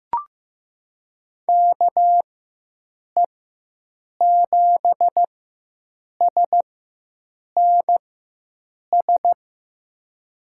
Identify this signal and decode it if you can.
Morse code, heard as a keyed tone, about 15 wpm, spaced as for 7 wpm, 700 Hz. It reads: KE7SNS
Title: USA Call Signs - 6 Characters - 4x Character Spacing - 15wpm